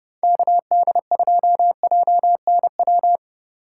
Morse code, heard as a keyed tone, about 32 wpm, 700 Hz. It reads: KD2JNW